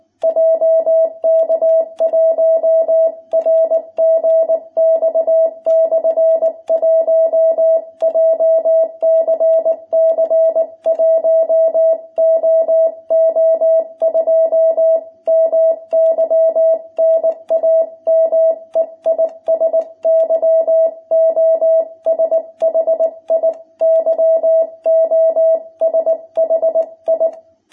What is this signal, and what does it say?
Morse code, heard as a keyed tone, about 19 wpm, 660 Hz. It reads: JK1RGX/1 JCC 1OO2 MY NAME IS YOSHI YOSHI
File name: jh8cbh / 2021cw